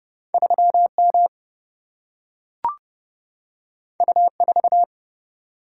Morse code, heard as a keyed tone, about 30 wpm, 700 Hz. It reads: 3M U4
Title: Single-Letter Single-Number 30wpm